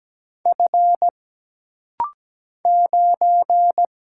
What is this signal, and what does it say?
Morse code, heard as a keyed tone, about 17 wpm, 700 Hz. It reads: F 9
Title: Single Letter-Number - Rapid-Fire 17wpm